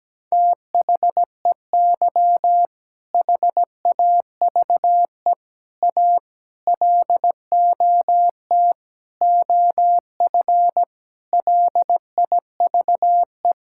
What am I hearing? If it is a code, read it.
Morse code, heard as a keyed tone, about 17 wpm, 700 Hz. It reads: THEY HAVE A LOT OF LIVE
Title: Sentences from Top 1000 Words - Review Part 5 - 17wpm